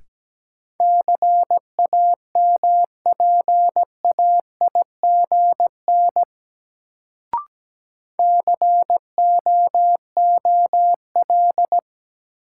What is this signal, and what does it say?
Morse code, heard as a keyed tone, about 17 wpm, 700 Hz. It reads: CAMPAIGN COOL
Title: Sentences from Top 1200 Words - New Word Drill - 17wpm